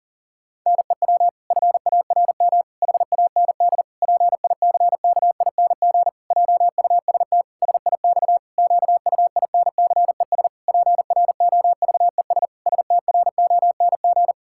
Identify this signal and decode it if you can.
Morse code, heard as a keyed tone, about 40 wpm, 700 Hz. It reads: NEW FARM HAND PICKING JUST SIX QUINCES PROVES STRONG
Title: Pangram Challenge v2 - 40wpm